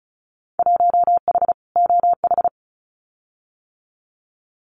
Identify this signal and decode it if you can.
Morse code, heard as a keyed tone, about 34 wpm, 700 Hz. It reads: 1H OH
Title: Winter Field Day Exchanges 35wpm